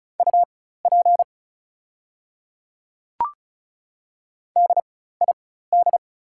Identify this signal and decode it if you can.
Morse code, heard as a keyed tone, about 35 wpm, spaced as for 16 wpm, 700 Hz. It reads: UP DID
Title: 4x Character Spacing - Top 100 Words - 35wpm